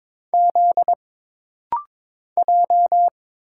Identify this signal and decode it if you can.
Morse code, heard as a keyed tone, about 22 wpm, 700 Hz. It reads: Z J
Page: This rapid-fire timing is geared to push your ICR (Instant Character Recognition) skills.